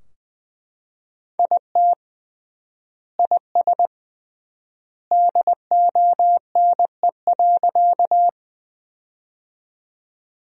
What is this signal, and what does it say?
Morse code, heard as a keyed tone, about 20 wpm, 700 Hz. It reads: IT IS DONE.
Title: 3x Word-Spacing - Sentences from Top 500 Words - 20wpm